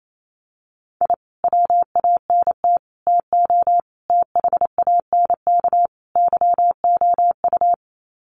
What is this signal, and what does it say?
Morse code, heard as a keyed tone, about 28 wpm, 700 Hz. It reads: I WANT TO THANK YOU
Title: Sets of 5 Words Encore 28wpm